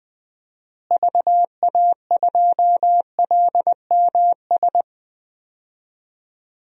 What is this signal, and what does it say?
Morse code, heard as a keyed tone, about 20 wpm, 700 Hz. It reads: VA2LMS
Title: DXCC-100 Call Signs - 6 Characters - Repeated 2x - 20wpm